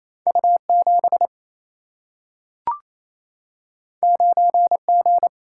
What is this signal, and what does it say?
Morse code, heard as a keyed tone, about 28 wpm, 700 Hz. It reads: U 7 9 G
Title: Single-Letter Single-Number 28wpm